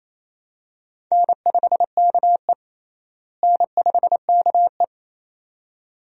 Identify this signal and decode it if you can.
Morse code, heard as a keyed tone, about 28 wpm, 700 Hz. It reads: N5KE N5KE
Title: USA Call Signs - 4 Characters - Repeated 2x - 28wpm